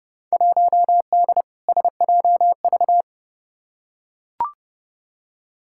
Morse code, heard as a keyed tone, about 30 wpm, 700 Hz. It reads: 1D SJV
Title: Field Day Exchanges Encore - 30wpm